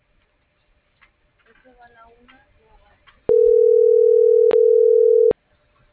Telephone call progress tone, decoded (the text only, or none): ringback tone